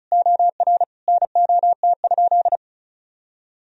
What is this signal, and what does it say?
Morse code, heard as a keyed tone, about 34 wpm, 700 Hz. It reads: OR NOT?
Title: Sentences from Top 1000 Words - Review Part 1 - 35wpm